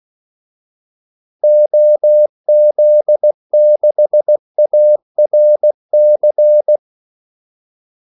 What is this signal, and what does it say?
Morse code, heard as a keyed tone, about 16 wpm, 600 Hz. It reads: OZ6ARC
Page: Kaldesignaler 16 wpm